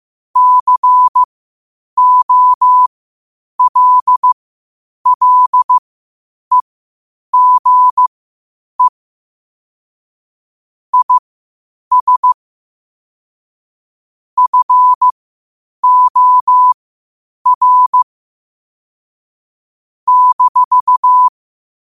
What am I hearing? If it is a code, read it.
Morse code, heard as a keyed tone, about 15 wpm, spaced as for 8 wpm, 1000 Hz. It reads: COLLEGE IS FOR -
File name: qotdfiles / 8wpm